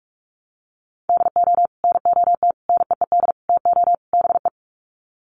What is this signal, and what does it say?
Morse code, heard as a keyed tone, about 44 wpm, 700 Hz. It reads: DO NOT NEED TO BE
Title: Sets of 5 Words Encore 45wpm